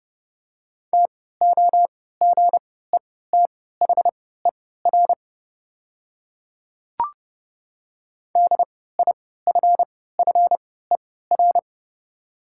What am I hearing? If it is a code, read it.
Morse code, heard as a keyed tone, about 30 wpm, spaced as for 17 wpm, 700 Hz. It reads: TOGETHER DIFFER
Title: 3x Character Spacing - Top 300 Words - 30wpm